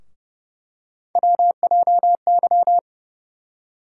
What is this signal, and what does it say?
Morse code, heard as a keyed tone, about 30 wpm, 700 Hz. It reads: WJY
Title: USA Call Sign Postfixes - 3 Letters 30wpm